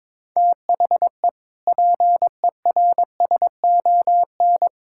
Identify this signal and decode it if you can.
Morse code, heard as a keyed tone, about 22 wpm, 700 Hz. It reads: THE PERSON
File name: practice / 22wpm / Sentences with 8-Word Phrases - 22wpm